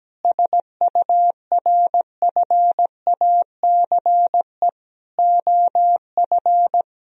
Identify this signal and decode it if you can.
Morse code, heard as a keyed tone, about 17 wpm, 700 Hz. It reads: SURFACE OF
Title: Sentences from Top 1000 Words - Review Part 3 - 17wpm